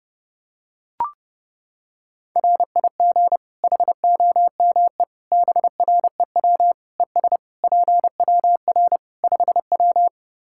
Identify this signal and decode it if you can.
Morse code, heard as a keyed tone, about 30 wpm, 700 Hz. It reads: RIG HOME BREW ES PWR 5W
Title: K1USN SST QSO Practice 30wpm